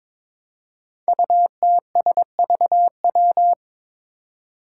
Morse code, heard as a keyed tone, about 22 wpm, 700 Hz. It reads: UTSVW
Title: Five Challenging Letter-Number Combinations 22wpm